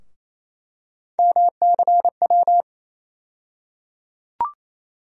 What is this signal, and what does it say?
Morse code, heard as a keyed tone, about 28 wpm, 700 Hz. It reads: MCW
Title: USA Call Sign Postfixes - 3 Letters 28wpm